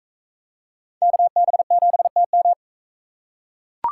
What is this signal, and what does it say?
Morse code, heard as a keyed tone, about 43 wpm, 700 Hz. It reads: KB7TM